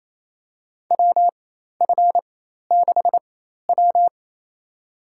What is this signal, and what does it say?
Morse code, heard as a keyed tone, about 28 wpm, spaced as for 13 wpm, 700 Hz. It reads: WF6W
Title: USA Call Signs - 4 Characters - 4x Character Spacing - 28wpm